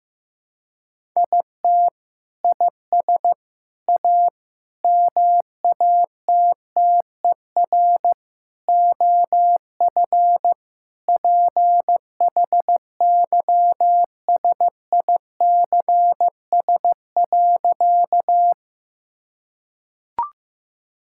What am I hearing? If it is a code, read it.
Morse code, heard as a keyed tone, about 15 wpm, 700 Hz. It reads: IT IS A MATTER OF PHYSICS.
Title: Sentences from Top 1000 Words - Review Part 5 - 15wpm